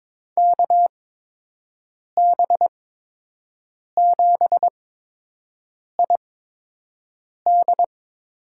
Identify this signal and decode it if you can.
Morse code, heard as a keyed tone, about 22 wpm, spaced as for 6 wpm, 700 Hz. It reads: KB7ID
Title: USA Call Signs - 6 Characters - 8x Character Spacing - 22wpm